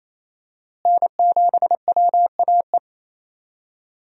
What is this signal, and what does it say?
Morse code, heard as a keyed tone, about 28 wpm, 700 Hz. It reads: N7WAE
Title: USA Call Signs - 5 Characters - Repeated 2x - 28wpm